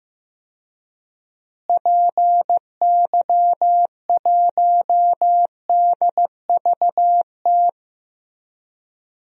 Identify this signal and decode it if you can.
Morse code, heard as a keyed tone, about 15 wpm, 700 Hz. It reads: PY1DVT
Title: DXCC-100 Call Signs - 6 Characters - Repeated 3x - 15wpm